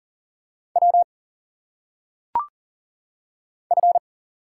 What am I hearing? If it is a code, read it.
Morse code, heard as a keyed tone, about 39 wpm, 700 Hz. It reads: W F
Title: Single Letters - Phonetic Pronunciation 40wpm